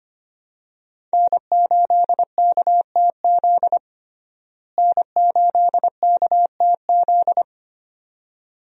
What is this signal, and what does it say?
Morse code, heard as a keyed tone, about 25 wpm, 700 Hz. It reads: N8KTZ N8KTZ
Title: USA Call Signs - 5 Characters - Repeated 2x - 25wpm